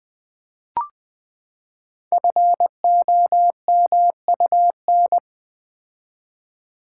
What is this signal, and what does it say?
Morse code, heard as a keyed tone, about 20 wpm, 700 Hz. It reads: FOMUN
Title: Code Groups - 5 Letters - 20wpm